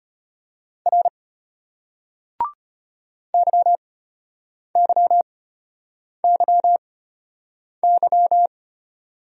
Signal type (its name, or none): Morse code